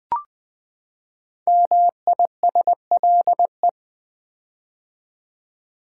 Morse code, heard as a keyed tone, about 20 wpm, 700 Hz. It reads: MISLE
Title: Build A Word - 5 Letters At A Time - 10-Letter Words - 20wpm